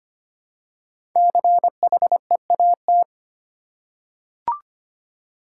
Morse code, heard as a keyed tone, about 25 wpm, 700 Hz. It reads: CHEAT
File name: 5 Letter Words - First 3 Letters Given - 25wpm